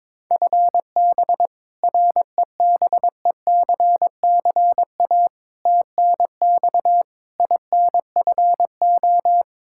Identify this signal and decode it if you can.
Morse code, heard as a keyed tone, about 22 wpm, 700 Hz. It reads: FB REBECCA TNX INFO
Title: Practice QSO - Exchange 3 - More Info - 22wpm